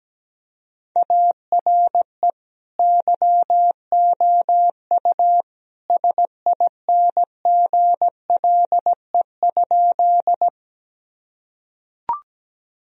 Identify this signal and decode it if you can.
Morse code, heard as a keyed tone, about 17 wpm, 700 Hz. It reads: ARE YOU SINGLE?
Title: Sentences from Top 1000 Words - Review Part 4 - 17wpm